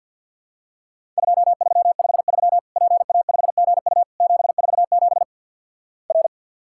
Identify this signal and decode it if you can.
Morse code, heard as a keyed tone, about 50 wpm, 700 Hz. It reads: 2353 PA5GU 747 R
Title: Worked All Europe DX Contest - QTC Exchange - 50wpm